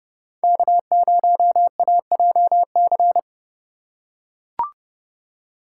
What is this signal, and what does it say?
Morse code, heard as a keyed tone, about 28 wpm, 700 Hz. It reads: K0AJC